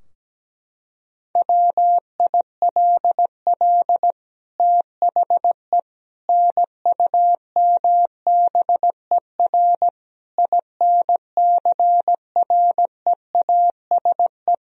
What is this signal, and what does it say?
Morse code, heard as a keyed tone, about 17 wpm, 700 Hz. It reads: WILL THE NUMBER INCREASE
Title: Sentences from Top 1000 Words - Review Part 4 - 17wpm